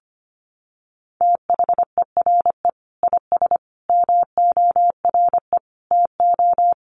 Morse code, heard as a keyed tone, about 25 wpm, 700 Hz. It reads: THERE IS MORE TO
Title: Sentences from Top 200 Words 25wpm